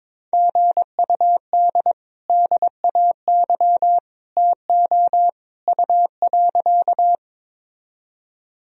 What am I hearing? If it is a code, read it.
Morse code, heard as a keyed tone, about 22 wpm, 700 Hz. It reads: GUD DAY TO U.
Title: Sentences from Top 200 Words and Abbreviations 22wpm